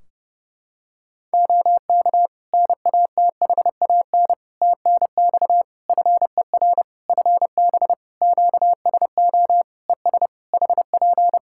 Morse code, heard as a keyed tone, about 30 wpm, 700 Hz. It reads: OK NATHAN TNX FER FB QSO ES HP
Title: Practice QSO - Exchange 4 - Ending - 30wpm